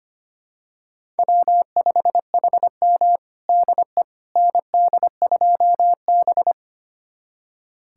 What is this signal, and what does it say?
Morse code, heard as a keyed tone, about 25 wpm, 700 Hz. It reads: W5HM DE ND2B
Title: USA Callsign DE Callsign - 4 Characters - 25wpm